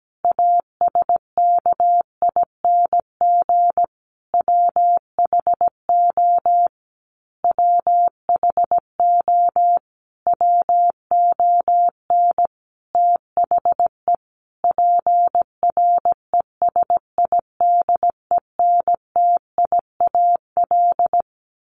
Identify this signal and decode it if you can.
Morse code, heard as a keyed tone, about 17 wpm, 700 Hz. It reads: ASKING WHO WHO WON THE PRESIDENTIAL
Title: A Sound Of Thunder 17wpm